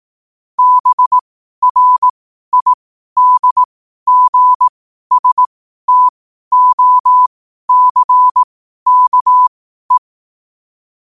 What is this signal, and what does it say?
Morse code, heard as a keyed tone, about 18 wpm, spaced as for 12 wpm, 1000 Hz. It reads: BRIDGSTOCKE
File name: qotdfiles / 12wpm